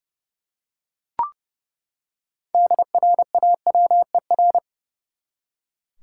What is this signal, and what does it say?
Morse code, heard as a keyed tone, about 30 wpm, 700 Hz. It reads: DRAWER
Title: Six Letter Words - 30wpm